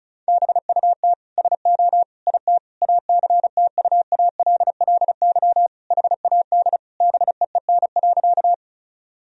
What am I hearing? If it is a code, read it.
Morse code, heard as a keyed tone, about 34 wpm, 700 Hz. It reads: BUT SO IT ACTUALLY HAD BEEN.
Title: Sentences from Top 1000 Words - Review Part 4 - 35wpm